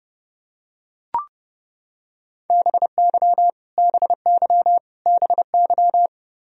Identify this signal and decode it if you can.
Morse code, heard as a keyed tone, about 32 wpm, 700 Hz. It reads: BY BY BY
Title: Top 100 Words Repeated 30wpm